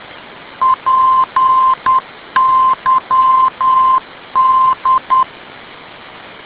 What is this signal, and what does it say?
Morse code, heard as a keyed tone, about 10 wpm, 1040 Hz. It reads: PYD
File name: pyd ndb